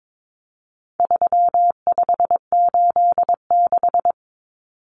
Morse code, heard as a keyed tone, about 22 wpm, 700 Hz. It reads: 3586
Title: Numbers - 4-Digit 22wpm